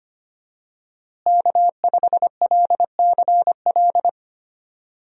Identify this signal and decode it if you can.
Morse code, heard as a keyed tone, about 25 wpm, 700 Hz. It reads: K5LCL
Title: USA Call Signs - 5 Characters - Repeated 3x - 25wpm